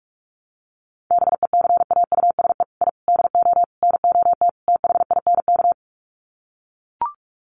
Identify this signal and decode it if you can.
Morse code, heard as a keyed tone, about 44 wpm, 700 Hz. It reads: BECAUSE I DO NOT THINK
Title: Sets of 5 Words Encore 45wpm